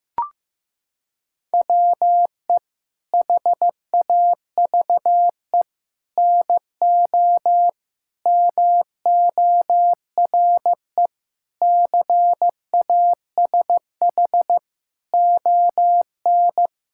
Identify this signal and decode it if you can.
Morse code, heard as a keyed tone, about 15 wpm, 700 Hz. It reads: WE HAVE NO MORE CASH ON
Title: Sentences from Top 1000 Words - Review Part 5 - 15wpm